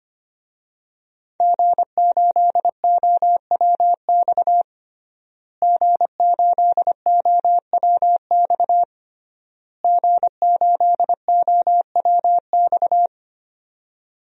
Morse code, heard as a keyed tone, about 25 wpm, 700 Hz. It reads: G8OWX G8OWX G8OWX
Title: DXCC-100 Call Signs - 5 Characters - Repeated 3x - 25wpm